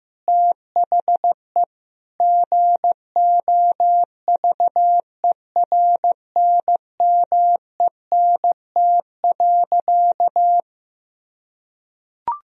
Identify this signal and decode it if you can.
Morse code, heard as a keyed tone, about 15 wpm, 700 Hz. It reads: THE GOVERNMENT.
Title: Sentences from Top 1000 Words - Review Part 4 - 15wpm